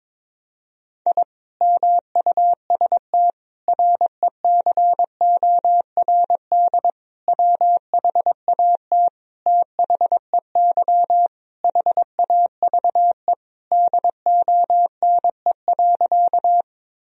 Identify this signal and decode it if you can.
Morse code, heard as a keyed tone, about 22 wpm, 700 Hz. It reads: I MUST RECORD WHAT THEY HAVE DONE.
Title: Sentences from Top 1000 Words - Review Part 3 - 22wpm